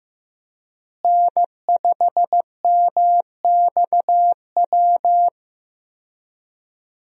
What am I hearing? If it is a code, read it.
Morse code, heard as a keyed tone, about 15 wpm, 700 Hz. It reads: N5MXW
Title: USA Call Signs - 5 Characters - Repeated 3x - 15wpm